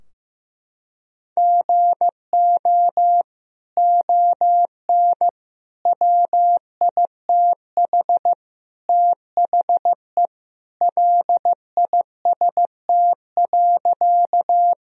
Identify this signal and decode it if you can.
Morse code, heard as a keyed tone, about 15 wpm, 700 Hz. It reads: GO ON WITH THE LIST.
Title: Sentences from Top 1000 Words - Review Part 2 - 15wpm